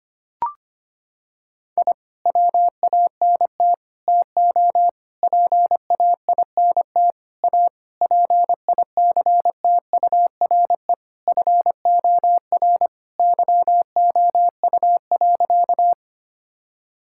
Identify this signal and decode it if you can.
Morse code, heard as a keyed tone, about 25 wpm, 700 Hz. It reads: I WANT TO PAINT A PICTURE FOR YOU.
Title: Sentences from Top 1000 Words - Review Part 6 - 25wpm